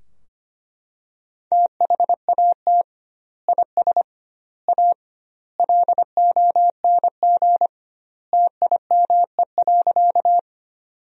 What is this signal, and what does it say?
Morse code, heard as a keyed tone, about 25 wpm, 700 Hz. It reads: THAT IS A LONG TIME.
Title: 2x Word-Spacing - Sentences from Top 100 Words - 25wpm